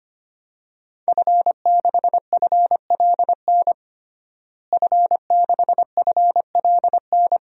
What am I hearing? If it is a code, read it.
Morse code, heard as a keyed tone, about 25 wpm, 700 Hz. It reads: F6FLN F6FLN
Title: DXCC-100 Call Signs - 5 Characters - Repeated 3x - 25wpm